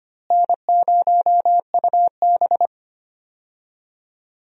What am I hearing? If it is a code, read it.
Morse code, heard as a keyed tone, about 25 wpm, 700 Hz. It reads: N0UB
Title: USA Call Signs - 4 Characters - Repeated 2x - 25wpm